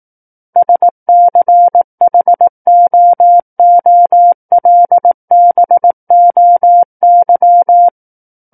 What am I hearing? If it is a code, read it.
Morse code, heard as a keyed tone, about 18 wpm, 700 Hz. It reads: SCHOOLBOY
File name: 18 WPM